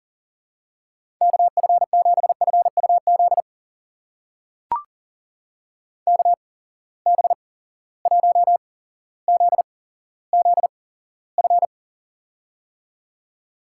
Morse code, heard as a keyed tone, about 40 wpm, spaced as for 17 wpm, 700 Hz. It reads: KF7FUZ KB1ZZF
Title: USA Call Signs - 6 Characters - 8x Character Spacing - 40wpm